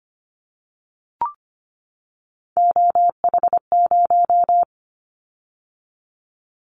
Morse code, heard as a keyed tone, about 25 wpm, 700 Hz. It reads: OH0
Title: Callsign Prefixes for Top 100 DXCC - 25wpm